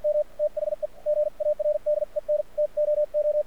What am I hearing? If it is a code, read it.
Morse code, heard as a keyed tone, about 53 wpm, spaced as for 44 wpm, 600 Hz, with radio noise. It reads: M THE GARDEN TOO
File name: archived / CRSnet